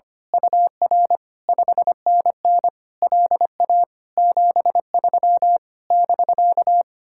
Morse code, heard as a keyed tone, about 24 wpm, 700 Hz. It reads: UR 5NN LA 73 <BK>
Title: Speed-Racing Common QSO Phrases - 25wpm